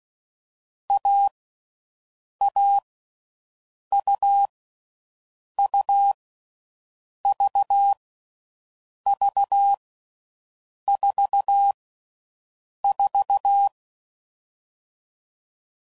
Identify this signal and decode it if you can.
Morse code, heard as a keyed tone, about 16 wpm, 800 Hz. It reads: A A U U V V 4 4
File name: Mixed / Short Runs / AUV AUV4